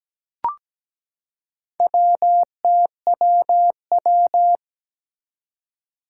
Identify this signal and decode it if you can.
Morse code, heard as a keyed tone, about 17 wpm, 700 Hz. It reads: WTWW
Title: Four Challenging Letter-Number Combinations 17wpm